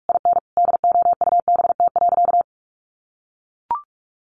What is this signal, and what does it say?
Morse code, heard as a keyed tone, about 44 wpm, 700 Hz. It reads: IN DOUBT.
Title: Sentences from Top 1500 Words 45wpm